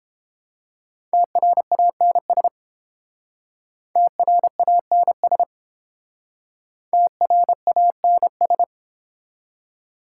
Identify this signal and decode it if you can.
Morse code, heard as a keyed tone, about 29 wpm, 690 Hz. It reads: TRANS TRANS TRANS